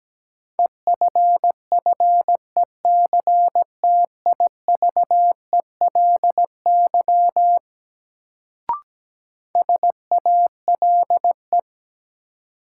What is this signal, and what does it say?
Morse code, heard as a keyed tone, about 17 wpm, 700 Hz. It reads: EFFECTIVELY SALE
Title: Sentences from Top 1400 Words - New Word Drill Alt - 17wpm